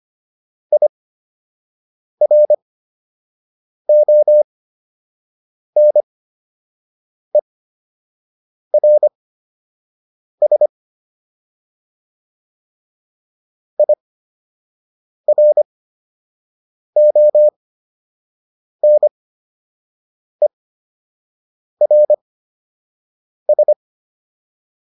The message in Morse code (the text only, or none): IRONERS IRONERS